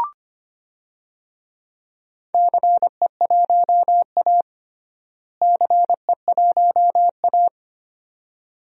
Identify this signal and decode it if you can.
Morse code, heard as a keyed tone, about 25 wpm, 700 Hz. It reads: CE1A CE1A
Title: DXCC-100 Call Signs - 4 Characters - Repeated 2x - 25wpm